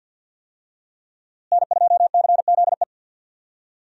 Morse code, heard as a keyed tone, about 47 wpm, 700 Hz. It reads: N2CZE
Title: USA Call Signs - 5 Characters - Repeated 2x - 50wpm